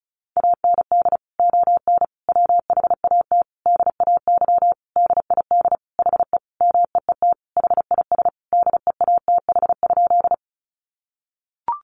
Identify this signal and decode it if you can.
Morse code, heard as a keyed tone, about 34 wpm, 700 Hz. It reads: AND ON WHAT DAY DID HE MEET HIS DEATH?
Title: Sentences from Top 1100 Words 35wpm